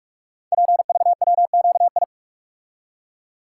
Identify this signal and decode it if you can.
Morse code, heard as a keyed tone, about 44 wpm, 690 Hz. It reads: PVWQI
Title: Code Groups - 5 Letters - 45wpm